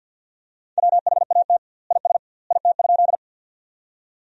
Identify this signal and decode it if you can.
Morse code, heard as a keyed tone, about 47 wpm, 700 Hz. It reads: WHAT IS IT?
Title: 2x Word-Spacing - Sentences from Top 100 Words - 50wpm